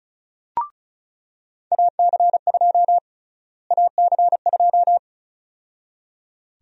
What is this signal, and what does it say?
Morse code, heard as a keyed tone, about 36 wpm, 700 Hz. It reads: AC2 AC2
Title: USA Call Sign Prefixes - Repeated 2x - 35wpm